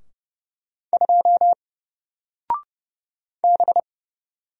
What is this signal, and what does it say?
Morse code, heard as a keyed tone, about 31 wpm, 700 Hz. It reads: 2 B
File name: Single-Letter-Number-30wpm